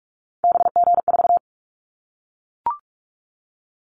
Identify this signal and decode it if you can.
Morse code, heard as a keyed tone, about 44 wpm, 700 Hz. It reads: BG4
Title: Callsign Prefixes for Top 100 DXCC - 45wpm